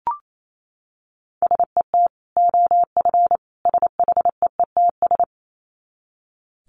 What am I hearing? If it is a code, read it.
Morse code, heard as a keyed tone, about 28 wpm, 700 Hz. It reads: SET OF SHEETS
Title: Dit Challenge 28wpm